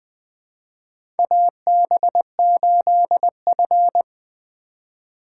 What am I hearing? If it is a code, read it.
Morse code, heard as a keyed tone, about 20 wpm, 700 Hz. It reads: AB8F
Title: USA Call Signs - 4 Characters - 2x Character Spacing - 20wpm